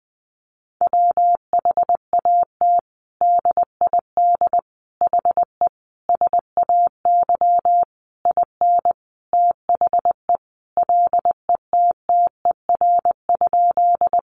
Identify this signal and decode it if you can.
Morse code, heard as a keyed tone, about 20 wpm, 700 Hz. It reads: WHAT DID HE SAY IN THE LETTER?
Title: Sentences from Top 300 Words 20wpm